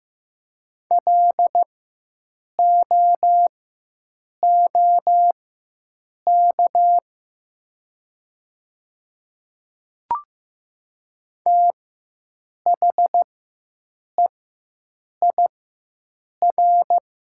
Morse code, heard as a keyed tone, about 15 wpm, spaced as for 7 wpm, 700 Hz. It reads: LOOK THEIR
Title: 4x Character Spacing - Top 300 Words - 15wpm